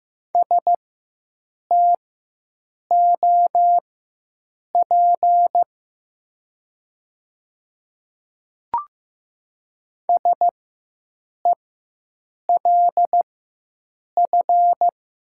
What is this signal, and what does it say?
Morse code, heard as a keyed tone, about 15 wpm, spaced as for 7 wpm, 700 Hz. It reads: STOP SELF
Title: 4x Character Spacing - Top 300 Words - 15wpm